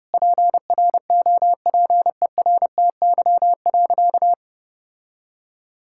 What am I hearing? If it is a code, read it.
Morse code, heard as a keyed tone, about 32 wpm, 700 Hz. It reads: PROPERTY.
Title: Sentences from Top 1600 Words 30wpm